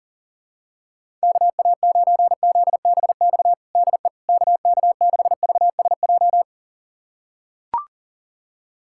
Morse code, heard as a keyed tone, about 40 wpm, 700 Hz. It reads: KA9ZBX DE KK6VSJ
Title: USA Callsign DE Callsign - 6 Characters - 40wpm